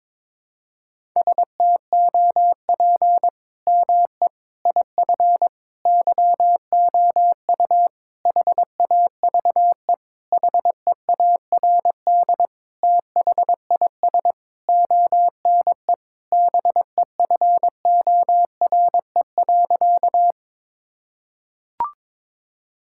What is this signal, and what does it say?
Morse code, heard as a keyed tone, about 22 wpm, 700 Hz. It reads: STOP ME IF YOU HAVE HEARD THIS ONE BEFORE.
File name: practice / 22wpm / Sentences with 8-Word Phrases - 22wpm